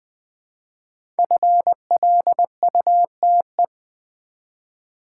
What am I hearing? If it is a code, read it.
Morse code, heard as a keyed tone, about 20 wpm, 700 Hz. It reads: FLUTE
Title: Easy Words with Hints - 20wpm